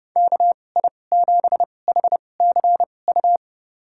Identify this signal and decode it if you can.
Morse code, heard as a keyed tone, about 30 wpm, spaced as for 22 wpm, 700 Hz. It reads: KI7HCU
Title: USA Call Signs - 6 Characters - 2x Character Spacing - 30wpm